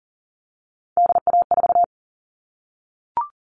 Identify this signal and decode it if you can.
Morse code, heard as a keyed tone, about 40 wpm, 700 Hz. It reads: DA4
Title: All Cut Numbers - 3-Digit 40wpm